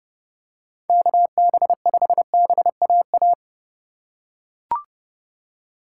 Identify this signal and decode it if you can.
Morse code, heard as a keyed tone, about 28 wpm, 700 Hz. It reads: KB5BAA